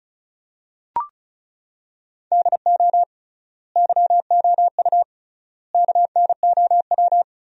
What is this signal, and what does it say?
Morse code, heard as a keyed tone, about 36 wpm, 700 Hz. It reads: DO YOU KNOW
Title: 3x Word-Spacing - Sentences from Top 100 Words - 35wpm